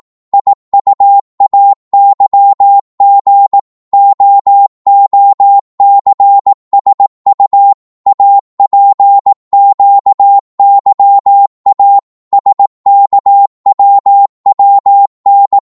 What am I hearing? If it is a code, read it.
Morse code, heard as a keyed tone, about 18 wpm, 800 Hz. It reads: IUAYG OOCSU APQYA SKWWN